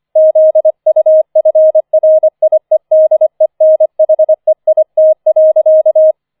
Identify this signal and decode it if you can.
Morse code, heard as a keyed tone, about 25 wpm, 630 Hz. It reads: ZUFRIEDENHEIT.
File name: 004 Vierter Streich 25wpm